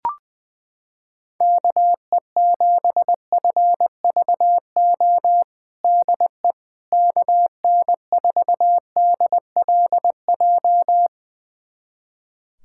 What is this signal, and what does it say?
Morse code, heard as a keyed tone, about 20 wpm, 700 Hz. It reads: KE7FVO DE KN4DLJ
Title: USA Callsign DE Callsign - 6 Characters - 20wpm